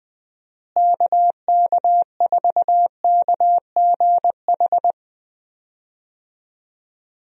Morse code, heard as a keyed tone, about 20 wpm, 700 Hz. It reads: KK4KGH
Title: USA Call Signs - 6 Characters - Repeated 2x - 20wpm